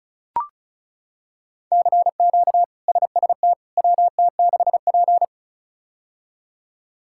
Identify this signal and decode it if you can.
Morse code, heard as a keyed tone, about 36 wpm, 700 Hz. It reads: CQ SST WT6P
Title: K1USN SST QSO Practice - 35wpm